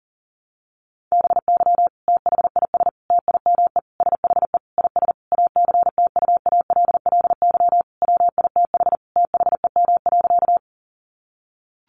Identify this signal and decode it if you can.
Morse code, heard as a keyed tone, about 40 wpm, 700 Hz. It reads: BY THIS TIME SHE IS ACTUALLY WITH THEM.
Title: Sentences from Top 700 Words 40wpm